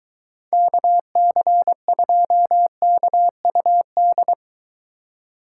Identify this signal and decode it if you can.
Morse code, heard as a keyed tone, about 23 wpm, 700 Hz. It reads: KC2KUD